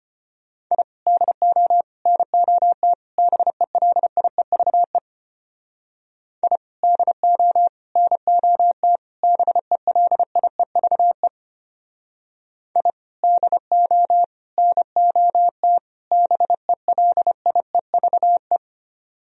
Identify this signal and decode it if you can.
Morse code, heard as a keyed tone, about 29 wpm, 700 Hz. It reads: I DO NOT BELIEVE I DO NOT BELIEVE I DO NOT BELIEVE
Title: Speed-Racing - Sets of 4 Words 25wpm